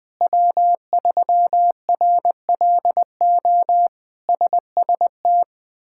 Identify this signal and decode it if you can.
Morse code, heard as a keyed tone, about 20 wpm, 700 Hz. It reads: W3RLO SST
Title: K1USN SST QSO Practice - 20wpm